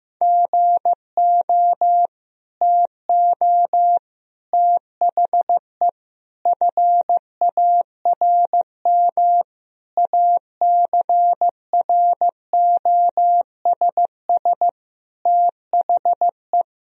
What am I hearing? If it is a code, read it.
Morse code, heard as a keyed tone, about 15 wpm, 700 Hz. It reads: GO TO THE FARM ACROSS THE
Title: Sentences from Top 1000 Words - Review Part 2 - 15wpm